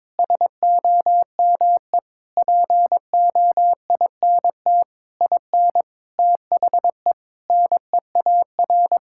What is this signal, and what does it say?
Morse code, heard as a keyed tone, about 22 wpm, 700 Hz. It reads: SOME POINT IN THE NEAR
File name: Sentences with 7-Word Phrases - 22wpm